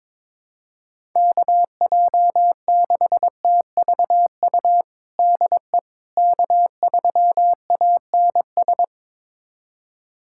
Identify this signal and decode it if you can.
Morse code, heard as a keyed tone, about 22 wpm, 700 Hz. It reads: KJ6TVU DE K3ANS
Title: USA Callsign DE Callsign 22wpm